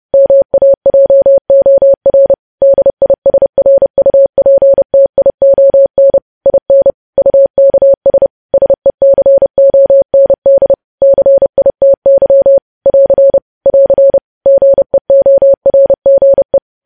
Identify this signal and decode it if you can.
Morse code, heard as a keyed tone, about 30 wpm, 550 Hz. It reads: MAJOR DISRUPTION IN UKS SECOND CITY <AR> <AR> GEORGE
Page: news headlines in morse code at 30 wpm